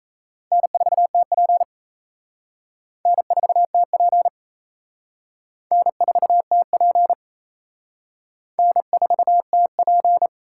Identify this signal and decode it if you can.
Morse code, heard as a keyed tone, about 34 wpm, 700 Hz. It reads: N4TP N4TP N4TP N4TP
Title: Speed Racing - USA 1x2 Call Signs 28wpm